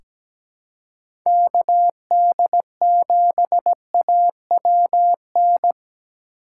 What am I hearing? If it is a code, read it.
Morse code, heard as a keyed tone, about 17 wpm, 700 Hz. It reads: KD7AWN
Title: USA Call Signs - 6 Characters - 2x Character Spacing - 17wpm